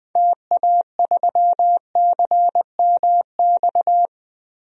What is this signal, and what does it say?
Morse code, heard as a keyed tone, about 20 wpm, 700 Hz. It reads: TA3CMX
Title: DXCC-100 Call Signs - 6 Characters - 20wpm